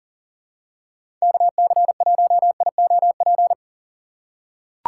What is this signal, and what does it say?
Morse code, heard as a keyed tone, about 40 wpm, 700 Hz. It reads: KC1IOP
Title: USA Call Signs - 3x Character Spacing - 40wpm